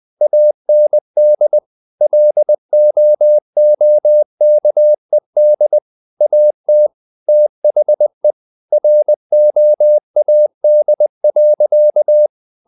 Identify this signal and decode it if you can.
Morse code, heard as a keyed tone, about 20 wpm, 600 Hz. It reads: AND LOOKED AT THE ROAD.
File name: the old man and the sea part 2 20wpm